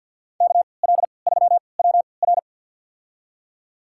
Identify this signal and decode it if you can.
Morse code, heard as a keyed tone, about 51 wpm, spaced as for 28 wpm, 700 Hz. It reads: KL3WR
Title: USA Call Signs - 5 Characters - 3x Character Spacing - 50wpm